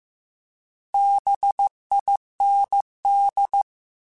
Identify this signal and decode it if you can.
Morse code, heard as a keyed tone, about 15 wpm, 800 Hz. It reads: BIND